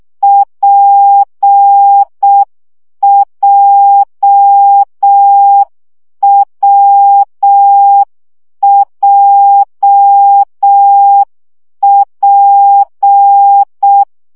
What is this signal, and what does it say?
Morse code, heard as a keyed tone, about 6 wpm, 800 Hz. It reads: PJWJP
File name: CW gyakorlas w p j 5 baud